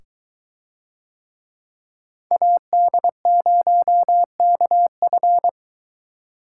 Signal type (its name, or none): Morse code